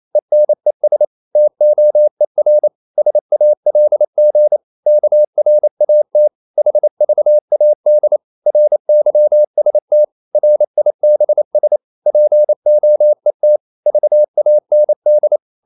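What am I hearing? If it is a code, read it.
Morse code, heard as a keyed tone, about 28 wpm, 600 Hz. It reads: ENES TOER SALG KRAT HVAD RYST RIBS POET VAND
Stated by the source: Danish short words sent in Morse at 28 wpm, from CW med Gnister.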